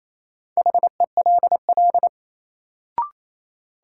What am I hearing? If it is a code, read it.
Morse code, heard as a keyed tone, about 28 wpm, 700 Hz. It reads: HELL
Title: Sets of 5 Words Encore - New Word Drill - 28wpm